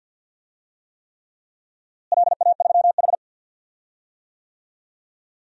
Morse code, heard as a keyed tone, about 53 wpm, 700 Hz. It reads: LA3H
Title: DXCC-100 Call Signs - 4 Characters - Repeated 3x - 50wpm